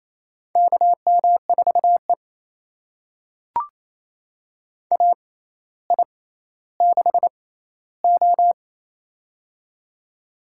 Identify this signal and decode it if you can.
Morse code, heard as a keyed tone, about 27 wpm, spaced as for 14 wpm, 700 Hz. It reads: KM4E AI6O
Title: USA Call Signs - 4 Characters - 6x Character Spacing - 28wpm